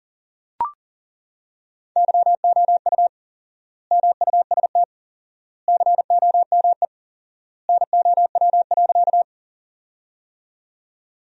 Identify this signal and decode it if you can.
Morse code, heard as a keyed tone, about 37 wpm, 700 Hz. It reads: YOU MUST COME NOW.
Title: 4x Word-Spacing - Sentences from Top 200 Words - 40wpm